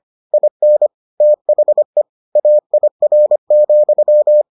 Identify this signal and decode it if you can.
Morse code, heard as a keyed tone, about 25 wpm, 600 Hz. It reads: IN THE AIR,